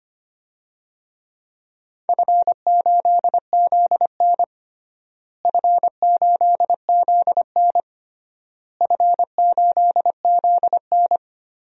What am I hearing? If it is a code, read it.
Morse code, heard as a keyed tone, about 25 wpm, 700 Hz. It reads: F8ZN F8ZN F8ZN
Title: DXCC-100 Call Signs - 4 Characters - Repeated 3x - 25wpm